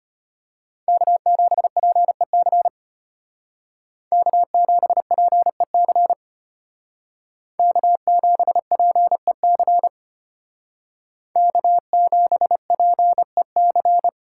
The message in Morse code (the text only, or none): K7PEC K7PEC K7PEC K7PEC